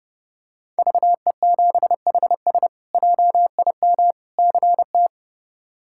Morse code, heard as a keyed tone, about 30 wpm, 700 Hz. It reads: VE7HS JIM CT
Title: K1USN SST QSO Practice - 30wpm